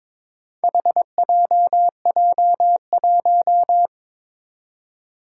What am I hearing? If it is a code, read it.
Morse code, heard as a keyed tone, about 22 wpm, 700 Hz. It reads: HJJ1
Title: Four Challenging Letter-Number Combinations 22wpm